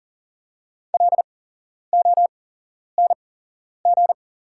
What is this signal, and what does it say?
Morse code, heard as a keyed tone, about 40 wpm, spaced as for 11 wpm, 690 Hz. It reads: LONG
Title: 8x Character Spacing - Top 100 Words - 40wpm